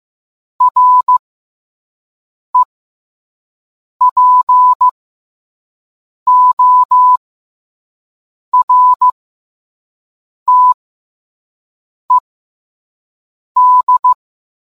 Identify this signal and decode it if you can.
Morse code, heard as a keyed tone, about 15 wpm, spaced as for 5 wpm, 1000 Hz. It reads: REPORTED